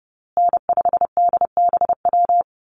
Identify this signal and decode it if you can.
Morse code, heard as a keyed tone, about 30 wpm, 700 Hz. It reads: N5DBW
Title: USA Call Signs Repeated 2x 30wpm